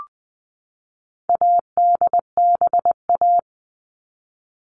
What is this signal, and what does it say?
Morse code, heard as a keyed tone, about 20 wpm, 700 Hz. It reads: ADBA
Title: All Cut Numbers - 4-Digit 20wpm